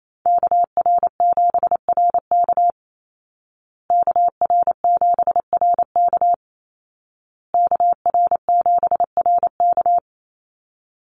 Morse code, heard as a keyed tone, about 28 wpm, 700 Hz. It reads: KR7RK KR7RK KR7RK
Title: Call Signs Repeated 2x - 28wpm